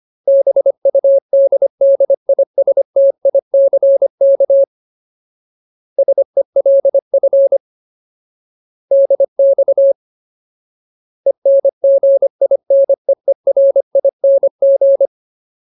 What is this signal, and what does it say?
Morse code, heard as a keyed tone, about 25 wpm, 550 Hz. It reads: BUDDISTICK SELF DX ENGINEERING
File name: Ultimate Word List / 25WPM Ultimate